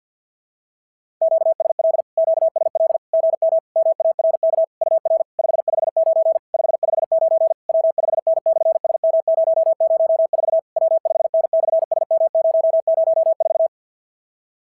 Morse code, heard as a keyed tone, about 46 wpm, 660 Hz. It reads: QSL QSL GM MARK UR 559 559 W5N/SM004 W5N/SM004
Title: SOTA QSO Practice 50wpm